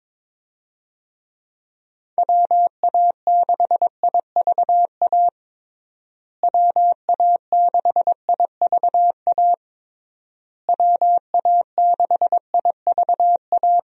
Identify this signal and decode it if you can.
Morse code, heard as a keyed tone, about 22 wpm, 700 Hz. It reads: WA6IVA WA6IVA WA6IVA
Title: USA Call Signs - 6 Characters - Repeated 3x - 22wpm